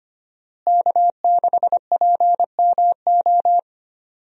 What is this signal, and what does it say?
Morse code, heard as a keyed tone, about 25 wpm, 700 Hz. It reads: K6PMO
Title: USA Call Signs - 5 Characters - Repeated 2x - 25wpm